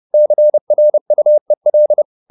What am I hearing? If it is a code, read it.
Morse code, heard as a keyed tone, about 30 wpm, 600 Hz. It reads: CRUEL
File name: the old man and the sea / the old man and the sea 1 30wpm